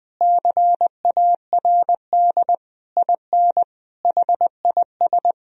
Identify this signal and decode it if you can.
Morse code, heard as a keyed tone, about 20 wpm, 700 Hz. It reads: CARD IN HIS
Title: Sentences from Top 1000 Words - Review Part 4 - 20wpm